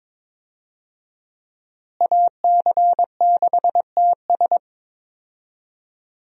Morse code, heard as a keyed tone, about 22 wpm, 700 Hz. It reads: AC6TS
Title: USA Call Signs - 5 Characters - Repeated 2x - 22wpm